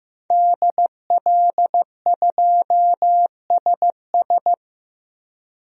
Morse code, heard as a keyed tone, about 15 wpm, 700 Hz. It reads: DL2SS
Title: DXCC-100 Call Signs - 5 Characters - Repeated 3x - 15wpm